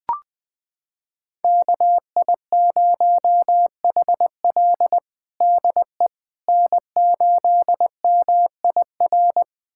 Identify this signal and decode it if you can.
Morse code, heard as a keyed tone, about 20 wpm, 700 Hz. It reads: KI0HL DE N8MIR
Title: USA Callsign DE Callsign - 5 Characters - 20wpm